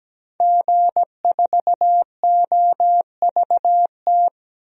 Morse code, heard as a keyed tone, about 17 wpm, 700 Hz. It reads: G4OVT
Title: DXCC-100 Call Signs - 5 Characters - Repeated 3x - 17wpm